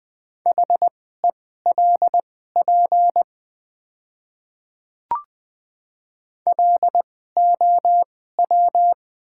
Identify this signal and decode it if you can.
Morse code, heard as a keyed tone, about 20 wpm, spaced as for 14 wpm, 700 Hz. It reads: HELP LOW
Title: 2x Character Spacing - Top 300 Words - 20wpm